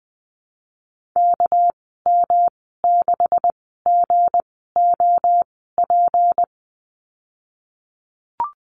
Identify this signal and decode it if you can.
Morse code, heard as a keyed tone, about 20 wpm, spaced as for 15 wpm, 700 Hz. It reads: KM6GOP
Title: 207 Lesson 33g - Letter j - Call Signs 20wpm